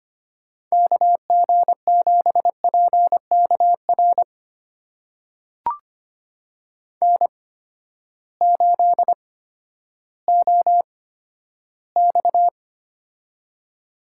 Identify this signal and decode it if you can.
Morse code, heard as a keyed tone, about 25 wpm, spaced as for 12 wpm, 700 Hz. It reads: KG7PKR N8OX
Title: USA Call Signs - 8x Character Spacing - 25wpm